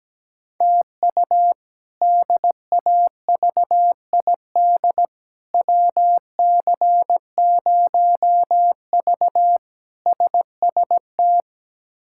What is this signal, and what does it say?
Morse code, heard as a keyed tone, about 17 wpm, 700 Hz. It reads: TU DAVID WC0V SST
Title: K1USN SST QSO Practice - 17wpm